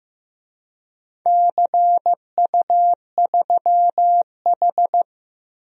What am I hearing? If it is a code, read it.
Morse code, heard as a keyed tone, about 15 wpm, 700 Hz. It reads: CU3H
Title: DXCC-100 Call Signs - 4 Characters - 15wpm